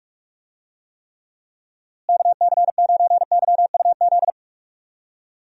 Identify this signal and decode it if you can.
Morse code, heard as a keyed tone, about 46 wpm, 700 Hz. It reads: KC9YUZ
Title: USA Call Signs - 6 Characters - Repeated 2x - 45wpm